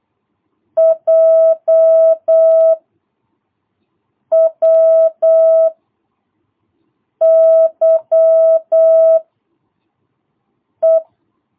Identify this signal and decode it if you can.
Morse code, heard as a keyed tone, about 8 wpm, 660 Hz. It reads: J W Y E
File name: jh8cbh / 2022cw